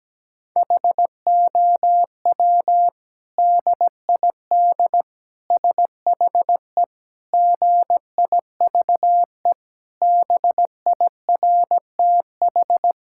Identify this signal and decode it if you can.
Morse code, heard as a keyed tone, about 17 wpm, 700 Hz. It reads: HOW DID SHE GIVE BIRTH
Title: Sentences from Top 1000 Words - Review Part 5 - 17wpm